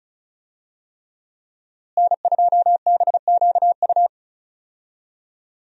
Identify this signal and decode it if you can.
Morse code, heard as a keyed tone, about 36 wpm, 700 Hz. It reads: N2BQU
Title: USA Call Signs - 5 Characters - Repeated 3x - 35wpm